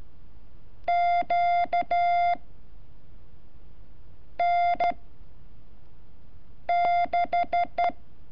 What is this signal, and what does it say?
Morse code, heard as a keyed tone, about 13 wpm, 700 Hz. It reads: Q N 6
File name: curso / ejercicio 30